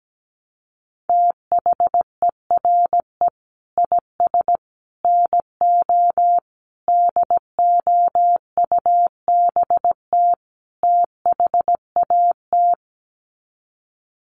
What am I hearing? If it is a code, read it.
Morse code, heard as a keyed tone, about 17 wpm, 700 Hz. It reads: THERE IS NO DOUBT THAT
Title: Sets of 5 Words 17wpm